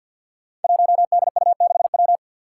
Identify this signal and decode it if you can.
Morse code, heard as a keyed tone, about 53 wpm, 700 Hz. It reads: 1DU6W
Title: Five Challenging Letter-Number Combinations 50wpm